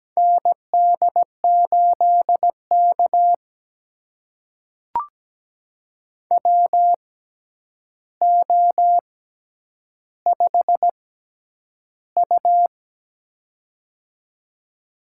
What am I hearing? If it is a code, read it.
Morse code, heard as a keyed tone, about 17 wpm, spaced as for 9 wpm, 700 Hz. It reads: ND8K WO5U
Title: USA Call Signs - 4 Characters - 6x Character Spacing - 17wpm